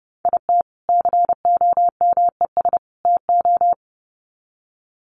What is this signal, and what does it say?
Morse code, heard as a keyed tone, about 30 wpm, 700 Hz. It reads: IT COMES TO
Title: Sets of 3 Words 30wpm